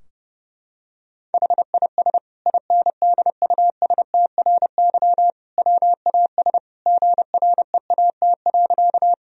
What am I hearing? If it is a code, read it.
Morse code, heard as a keyed tone, about 32 wpm, 700 Hz. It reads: HIS INDUSTRY WAS GREAT.
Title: Sentences from Top 1000 Words - Review Part 4 - 30wpm